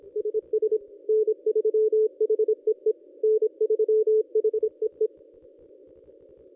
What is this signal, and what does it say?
Morse code, heard as a keyed tone, about 25 wpm, 430 Hz. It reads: SS N3HEE N3HEE
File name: contests / ss-cw-k5tr-op-n5rz -1100